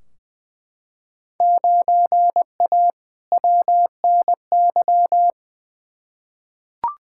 Morse code, heard as a keyed tone, about 20 wpm, 700 Hz. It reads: 9A WNY
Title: Field Day Exchanges Encore - 20wpm